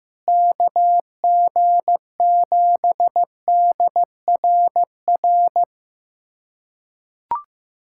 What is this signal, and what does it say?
Morse code, heard as a keyed tone, about 15 wpm, 700 Hz. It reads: KG7DRR
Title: USA Callsign DE Callsign 15wpm